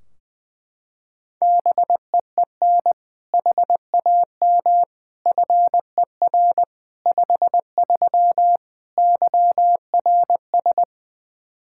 Morse code, heard as a keyed tone, about 21 wpm, 700 Hz. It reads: BEEN HAM FER 53 YRS
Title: Practice QSO - Exchange 3 - More Info - 20wpm